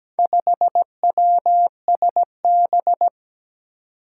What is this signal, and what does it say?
Morse code, heard as a keyed tone, about 17 wpm, 700 Hz. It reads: 5WSB
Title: Four Challenging Letter-Number Combinations 17wpm